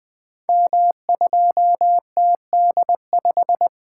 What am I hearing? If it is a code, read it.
Morse code, heard as a keyed tone, about 20 wpm, 700 Hz. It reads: M2TD5
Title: Code Groups - 5 Letter:Numbers - 20wpm